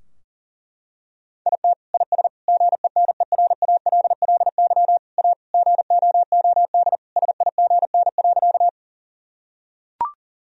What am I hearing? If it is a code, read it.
Morse code, heard as a keyed tone, about 40 wpm, 700 Hz. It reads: IT IS GENERALLY A GOOD SIGN.
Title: Sentences from Top 1000 Words - Review Part 4 - 40wpm